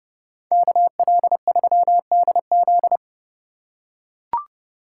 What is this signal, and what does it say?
Morse code, heard as a keyed tone, about 28 wpm, 700 Hz. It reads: KL3DZ